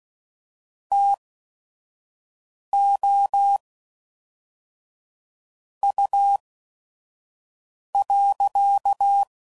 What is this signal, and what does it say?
Morse code, heard as a keyed tone, about 16 wpm, spaced as for 5 wpm, 800 Hz. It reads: TO U.
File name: CW Code Groups @5WPM / QSOs / QSO WA7QOX 1